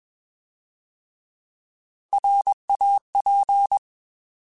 Morse code, heard as a keyed tone, about 21 wpm, 800 Hz. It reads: RAP